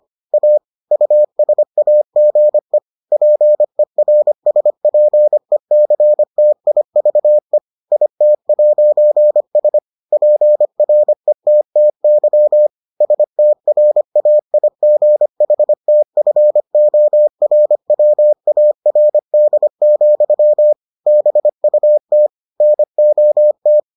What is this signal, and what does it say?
Morse code, heard as a keyed tone, about 25 wpm, 600 Hz. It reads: A USAGE PERSPECTIVE IT'S PRETTY STRAIGHTFORWARD, BUT NOT